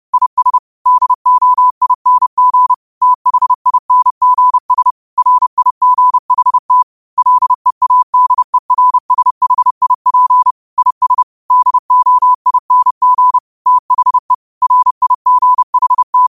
Morse code, heard as a keyed tone, about 30 wpm, 1000 Hz. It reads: IS DOING THINGS RIGHT LEADERSHIP IS DOING THE RIGHT
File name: qotdfiles / 30wpm